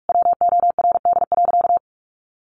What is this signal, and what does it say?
Morse code, heard as a keyed tone, about 44 wpm, 700 Hz. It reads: WORD.
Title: A Sound Of Thunder 45wpm